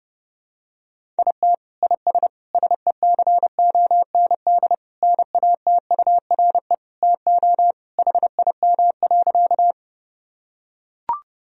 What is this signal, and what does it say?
Morse code, heard as a keyed tone, about 30 wpm, 700 Hz. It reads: IT IS SECOND NATURE TO HIM.
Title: Sentences from Top 1000 Words - Review Part 1 - 30wpm